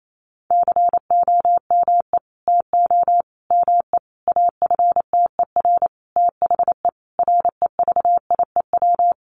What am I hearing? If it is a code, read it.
Morse code, heard as a keyed tone, about 28 wpm, 690 Hz. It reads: COME TO ME AFTER THE REVIEW
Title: Sentences from Top 900 Words 28wpm